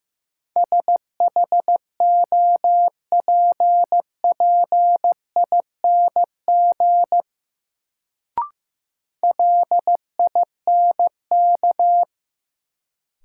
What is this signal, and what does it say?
Morse code, heard as a keyed tone, about 15 wpm, 700 Hz. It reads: SHOPPING LINK
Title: Sentences from Top 1000 Words - Review Part 7 - New Word Drill - 15wpm